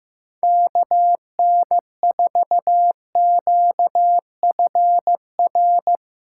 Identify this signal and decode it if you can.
Morse code, heard as a keyed tone, about 15 wpm, 700 Hz. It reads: KN4QFR
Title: Speed Racing - USA Call Signs 15wpm